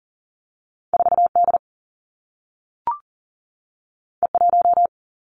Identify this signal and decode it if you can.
Morse code, heard as a keyed tone, about 38 wpm, 700 Hz. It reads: 4 D E 1
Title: Two Challenging Letter-Number Combinations 40wpm